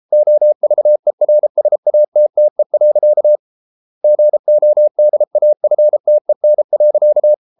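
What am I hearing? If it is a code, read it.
Morse code, heard as a keyed tone, about 33 wpm, 600 Hz. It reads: OVERSATTE. GODAFTEN.
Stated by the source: Morse code at 33 wpm, QRQ.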